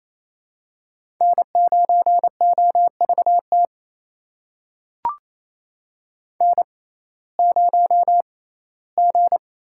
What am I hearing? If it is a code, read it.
Morse code, heard as a keyed tone, about 28 wpm, spaced as for 17 wpm, 700 Hz. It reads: N9OVT N0G
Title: USA Call Signs - 5 Characters - 6x Character Spacing - 28wpm